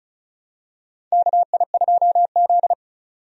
Morse code, heard as a keyed tone, about 36 wpm, 700 Hz. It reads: KI2Z
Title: USA Call Signs - 4 Characters - 35wpm